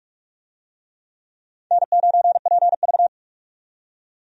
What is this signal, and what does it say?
Morse code, heard as a keyed tone, about 46 wpm, 700 Hz. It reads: N9PV
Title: USA Call Signs - 4 Characters - 45wpm